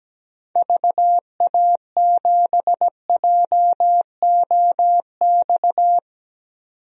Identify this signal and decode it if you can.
Morse code, heard as a keyed tone, about 17 wpm, 700 Hz. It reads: VA7JOX
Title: DXCC-100 Call Signs - 6 Characters - 17wpm